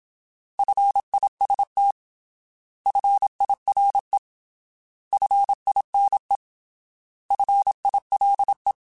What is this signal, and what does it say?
Morse code, heard as a keyed tone, about 27 wpm, 800 Hz. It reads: FIST FIRE FINE FILE
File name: Alphabetical / Alphabetical mp3s / Fs @25